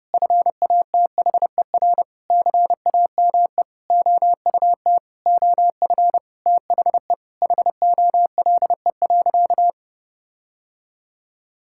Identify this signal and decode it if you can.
Morse code, heard as a keyed tone, about 32 wpm, 700 Hz. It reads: FATHER CAME OUT OF THE HOLE.
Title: Sentences from Top 1600 Words 30wpm